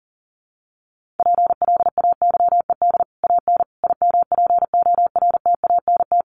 Morse code, heard as a keyed tone, about 40 wpm, 700 Hz. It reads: PLAYED AN IMPORTANT
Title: Sets of 5 Words Encore2 40wpm